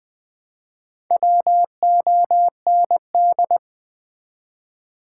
Morse code, heard as a keyed tone, about 21 wpm, 700 Hz. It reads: WOND
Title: Build A Word - 2 Letters At A Time - 6-Letter Words - 20wpm